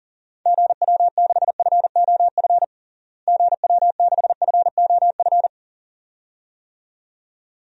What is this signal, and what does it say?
Morse code, heard as a keyed tone, about 40 wpm, 700 Hz. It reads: GW6FOF GW6FOF
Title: DXCC-100 Call Signs - 6 Characters - Repeated 2x - 40wpm